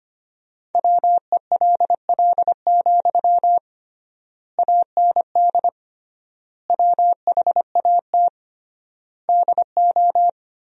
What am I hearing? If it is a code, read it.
Morse code, heard as a keyed tone, about 25 wpm, 700 Hz. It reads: WELL, AND WHAT DO
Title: 3x Word-Spacing - Sentences from Top 200 Words - 25wpm